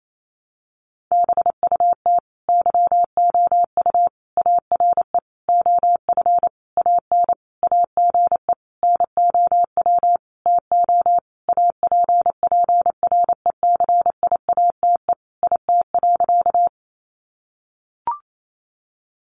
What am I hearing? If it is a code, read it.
Morse code, heard as a keyed tone, about 28 wpm, 700 Hz. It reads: BUT YOU ARE OF AN AGE NOW TO APPRECIATE IT.
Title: Sentences from Top 1300 Words 28wpm